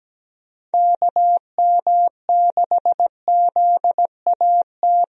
Morse code, heard as a keyed tone, about 17 wpm, 700 Hz. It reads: KM6ZAT
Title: USA Call Signs - 6 Characters - Repeated 2x - 17wpm